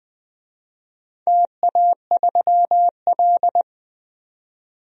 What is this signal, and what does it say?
Morse code, heard as a keyed tone, about 20 wpm, 700 Hz. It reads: TA3L
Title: DXCC-100 Call Signs - 4 Characters - 20wpm